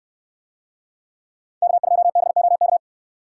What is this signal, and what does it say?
Morse code, heard as a keyed tone, about 69 wpm, 700 Hz. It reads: K2BOK